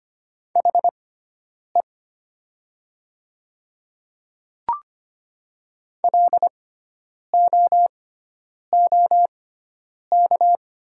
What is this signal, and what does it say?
Morse code, heard as a keyed tone, about 25 wpm, spaced as for 9 wpm, 700 Hz. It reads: HE LOOK
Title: 6x Character Spacing - Top 500 Words - 25wpm